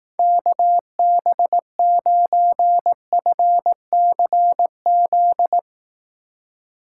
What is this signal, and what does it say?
Morse code, heard as a keyed tone, about 18 wpm, 700 Hz. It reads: KB9FCZ
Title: Speed Racing - USA Call Signs 15wpm